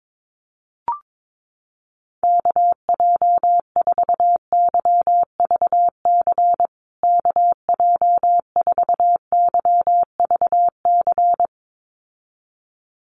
Morse code, heard as a keyed tone, about 22 wpm, 700 Hz. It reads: KJ4YVC KJ4YVC
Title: USA Call Signs Repeated 2x 22wpm